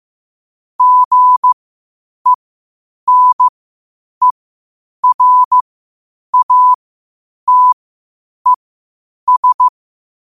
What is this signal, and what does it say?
Morse code, heard as a keyed tone, about 15 wpm, spaced as for 8 wpm, 1000 Hz. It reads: GENERATES
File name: qotdfiles / 8wpm